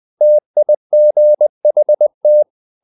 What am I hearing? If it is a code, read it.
Morse code, heard as a keyed tone, about 20 wpm, 600 Hz. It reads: TIGHT
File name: the old man and the sea part 2 20wpm